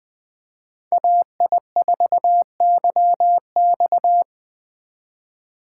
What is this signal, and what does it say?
Morse code, heard as a keyed tone, about 20 wpm, 700 Hz. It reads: AI4YX
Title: Speed Racing - USA Call Signs 17wpm